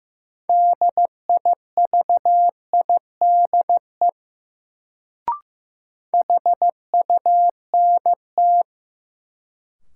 Morse code, heard as a keyed tone, about 15 wpm, 700 Hz. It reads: DIVIDE HUNT
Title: Sentences from Top 1600 Words - New Word Drill - 15wpm